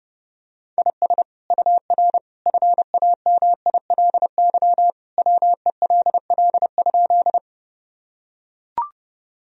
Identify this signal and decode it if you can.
Morse code, heard as a keyed tone, about 32 wpm, 700 Hz. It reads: IS UR FAMILY WELL?
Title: Sentences from Top 400 Words and Abbreviations 30wpm